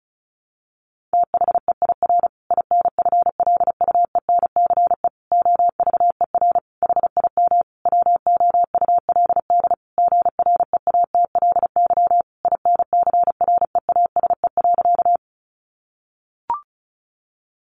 Morse code, heard as a keyed tone, about 34 wpm, 700 Hz. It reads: THEIR INFLUENCE OVER HIM WOULD GREATLY INCREASE.
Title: Sentences from Top 1400 Words 35wpm